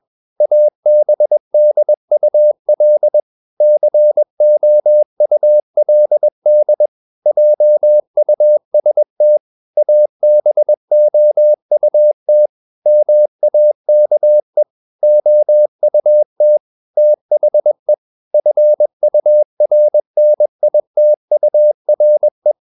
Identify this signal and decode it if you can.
Morse code, heard as a keyed tone, about 21 wpm, 600 Hz. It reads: ABDUL COULD JUST ABOUT MAKE OUT THE FURNITURE
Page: CW med Gnister